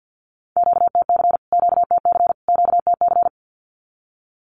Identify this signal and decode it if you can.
Morse code, heard as a keyed tone, about 50 wpm, 700 Hz. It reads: QTC QTC QTC
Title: Q-Codes Repeated 3x 50wpm